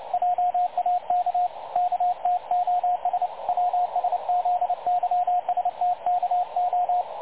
Morse code, heard as a keyed tone, about 29 wpm, 720 Hz, with radio noise. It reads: JAK KTOS WSZYSTKO
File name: dlazaawansowanych / nagraniazpasma / websdr recording 3565.0kHz